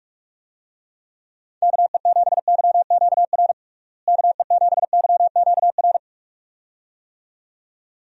Morse code, heard as a keyed tone, about 41 wpm, 700 Hz. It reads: KE7YQR KE7YQR
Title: USA Call Signs - 6 Characters - Repeated 2x - 45wpm